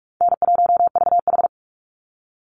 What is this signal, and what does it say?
Morse code, heard as a keyed tone, about 44 wpm, 700 Hz. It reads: N1VH
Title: Easy Call Signs Repeated 2x - 45wpm